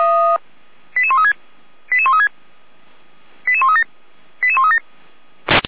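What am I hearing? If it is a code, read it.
Five-tone selective call (ZVEI), heard as quick runs of five tones, tones of about 70 ms. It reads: 88136 88136 88137 88137
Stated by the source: ZVEI five-tone call 88531 received by SDR, channel 455.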